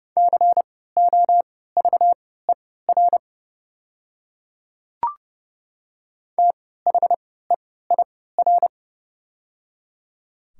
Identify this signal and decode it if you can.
Morse code, heard as a keyed tone, about 30 wpm, spaced as for 17 wpm, 700 Hz. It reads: COVER THEIR
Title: 3x Character Spacing - Top 300 Words - 30wpm